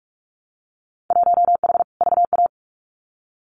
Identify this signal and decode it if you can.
Morse code, heard as a keyed tone, about 46 wpm, 700 Hz. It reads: 1H VA
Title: Winter Field Day Exchanges 45wpm